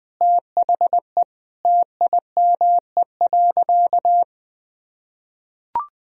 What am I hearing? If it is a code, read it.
Morse code, heard as a keyed tone, about 21 wpm, 700 Hz. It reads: THE TIME.
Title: Sentences from Top 200 Words and Abbreviations 20wpm